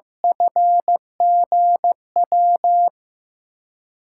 Morse code, heard as a keyed tone, about 16 wpm, 700 Hz. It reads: FGW
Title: USA Call Signs - 6 Characters - 6x Character Spacing - 15wpm